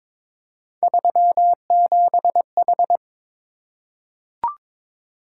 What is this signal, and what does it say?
Morse code, heard as a keyed tone, about 22 wpm, 700 Hz. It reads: 37H
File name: Three Letter-Number Combinations - 22wpm